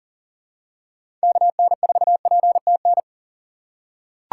Morse code, heard as a keyed tone, about 40 wpm, 700 Hz. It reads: KN4PTN
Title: USA Call Signs - 4x Character Spacing - 40wpm